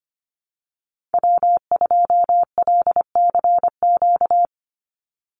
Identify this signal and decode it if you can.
Morse code, heard as a keyed tone, about 25 wpm, 700 Hz. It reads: W2LCQ
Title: Call Signs Repeated 3x - 25wpm